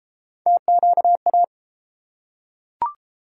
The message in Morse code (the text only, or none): TQA